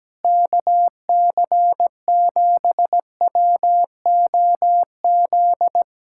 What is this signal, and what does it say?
Morse code, heard as a keyed tone, about 17 wpm, 700 Hz. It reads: KC7WOZ
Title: Speed-Racing Common QSO Phrases Encore - 17wpm